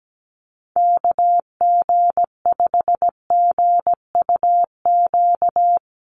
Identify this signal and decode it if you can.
Morse code, heard as a keyed tone, about 17 wpm, 700 Hz. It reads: KG5GUQ
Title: USA Call Signs Repeated 3x 17wpm